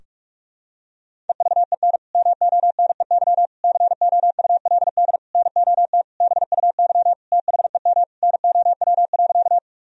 Morse code, heard as a keyed tone, about 47 wpm, 700 Hz. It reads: EVEN MONEY COULD NOT BUY THEM NOW.
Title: Sentences from Top 1000 Words - Review Part 4 - 45wpm